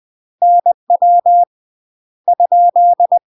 Morse code, heard as a keyed tone, about 20 wpm, 700 Hz. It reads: NW ?
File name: UK MASTER QSOs TRAINING/20WPM